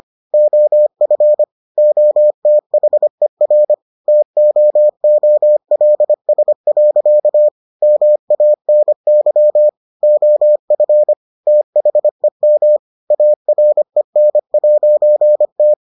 Morse code, heard as a keyed tone, about 25 wpm, 600 Hz. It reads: OF OTHER TOOLS. MANY OF THEM AREN'T